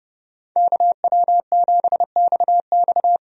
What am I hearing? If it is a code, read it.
Morse code, heard as a keyed tone, about 30 wpm, 700 Hz. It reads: KW7XX
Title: USA Call Signs - 5 Characters - 30wpm